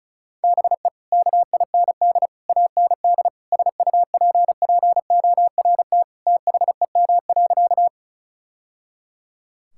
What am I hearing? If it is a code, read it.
Morse code, heard as a keyed tone, about 34 wpm, 700 Hz. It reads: BE KIND AND SUPPORT THEM.
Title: Sentences from Top 1000 Words - Review Part 5 - 35wpm